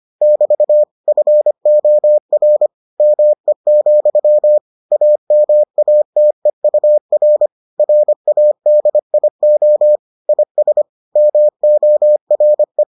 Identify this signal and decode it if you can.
Morse code, heard as a keyed tone, about 25 wpm, 600 Hz. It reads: <BT> FOR ME, AMATEUR RADIO IS MORE